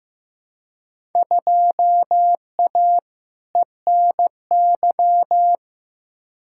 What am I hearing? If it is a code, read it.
Morse code, heard as a keyed tone, about 15 wpm, 700 Hz. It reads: 2A ENY
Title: Field Day Exchanges - 15wpm